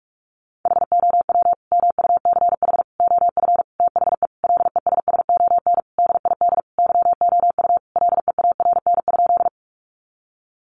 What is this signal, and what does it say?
Morse code, heard as a keyed tone, about 44 wpm, 700 Hz. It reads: HOW MUCH OF THE LESSON DID YOU LEARN?
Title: Sentences from Top 1200 Words 45wpm